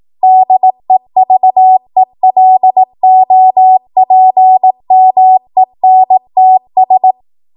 Morse code, heard as a keyed tone, about 18 wpm, 750 Hz. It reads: DEVELOPMENTS
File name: Archive / 18 WPM / 18WPM